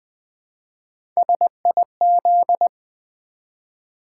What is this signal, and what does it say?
Morse code, heard as a keyed tone, about 20 wpm, 700 Hz. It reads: SIZ
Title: Build A Word - 1 Letter At A Time - 3 to 5-Letter Words - 20wpm